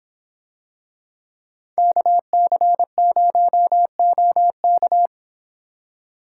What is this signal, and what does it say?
Morse code, heard as a keyed tone, about 25 wpm, 700 Hz. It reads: KC0OK